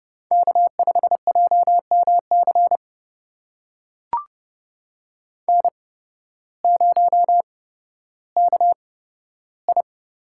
Morse code, heard as a keyed tone, about 31 wpm, spaced as for 14 wpm, 700 Hz. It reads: K5JMC N0KI
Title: USA Call Signs - 8x Character Spacing - 30wpm